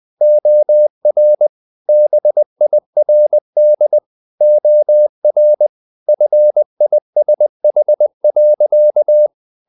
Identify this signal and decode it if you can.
Morse code, heard as a keyed tone, about 20 wpm, 600 Hz. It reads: OR BIRD OR FISH.
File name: the old man and the sea / the old man and the sea part 1 20wpm